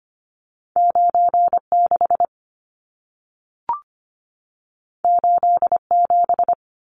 Morse code, heard as a keyed tone, about 25 wpm, 700 Hz. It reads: 96 87
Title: Numbers - 2-Digit 25wpm